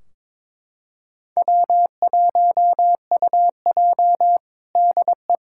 Morse code, heard as a keyed tone, about 22 wpm, 700 Hz. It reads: W1UJ DE
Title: K1USN SST QSO Practice 22wpm